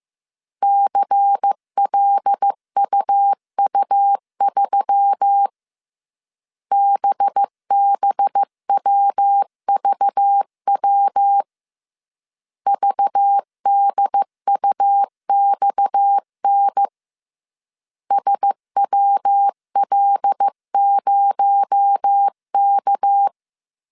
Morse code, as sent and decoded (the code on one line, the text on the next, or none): -.-. .-.. ..- ..- ...-- / -... -... .-- ...- .-- / ...- -.. ..- -..- -. / ... .-- .-.. ----- -.-
CLUU3 BBWVW VDUXN SWL0K